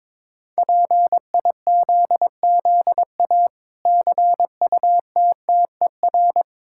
Morse code, heard as a keyed tone, about 22 wpm, 700 Hz. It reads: PIZZA CUTTER
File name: Difficult Words with Hints - 22wpm